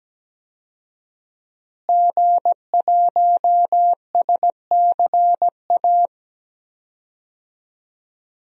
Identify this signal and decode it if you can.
Morse code, heard as a keyed tone, about 17 wpm, 700 Hz. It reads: G1SCA
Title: DXCC-100 Call Signs - 5 Characters - Repeated 3x - 17wpm